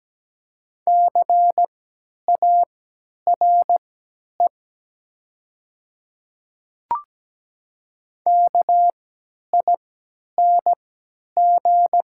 Morse code, heard as a keyed tone, about 17 wpm, spaced as for 10 wpm, 700 Hz. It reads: CARE KING
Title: 3x Character Spacing - Top 500 Words - 17wpm